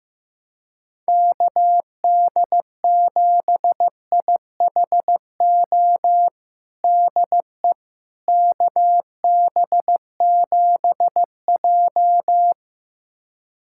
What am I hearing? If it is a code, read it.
Morse code, heard as a keyed tone, about 15 wpm, 700 Hz. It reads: KD7IHO DE KB7J
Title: USA Callsign DE Callsign 15wpm